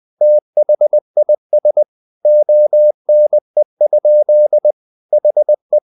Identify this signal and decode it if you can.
Morse code, heard as a keyed tone, about 20 wpm, 600 Hz. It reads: THIS ONE? HE
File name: the old man and the sea / the old man and the sea part 1 20wpm